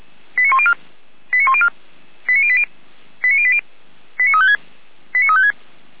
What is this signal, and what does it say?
Five-tone selective call (ZVEI), heard as quick runs of five tones, tones of about 70 ms. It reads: 89104 89104 89080 89080 89356 89356